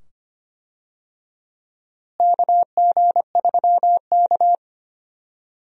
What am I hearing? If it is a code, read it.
Morse code, heard as a keyed tone, about 25 wpm, 700 Hz. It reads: KG3K
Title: USA Call Signs - 4 Characters - Repeated 2x - 25wpm